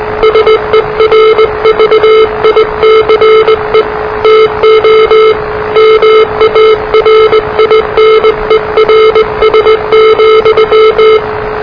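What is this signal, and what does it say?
Morse code, heard as a keyed tone, about 20 wpm, 440 Hz, with noise. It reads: SERVICE TO MARINERS,
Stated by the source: Morse challenge from NZ Net, the national amateur radio net of Aotearoa/New Zealand, which meets on 3535.0 kHz.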